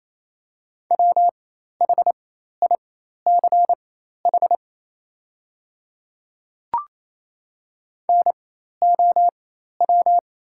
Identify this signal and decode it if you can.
Morse code, heard as a keyed tone, about 28 wpm, spaced as for 13 wpm, 700 Hz. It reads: WHICH NOW
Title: 4x Character Spacing - Top 500 Words - 28wpm